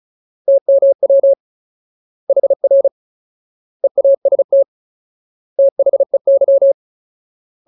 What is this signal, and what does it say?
Morse code, heard as a keyed tone, about 34 wpm, 550 Hz. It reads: TMW HR EAST THEY
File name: Ultimate Word List / 35WPM Ultimate